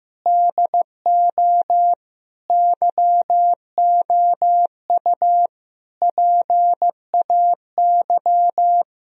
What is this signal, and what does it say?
Morse code, heard as a keyed tone, about 16 wpm, 700 Hz. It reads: DO YOU PAY
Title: Sentences from Top 1000 Words - Review Part 2 - 15wpm